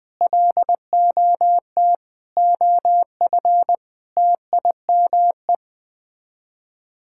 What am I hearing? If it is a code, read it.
Morse code, heard as a keyed tone, about 20 wpm, 700 Hz. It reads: LOT OF TIME
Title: Sets of 6 Words - 20wpm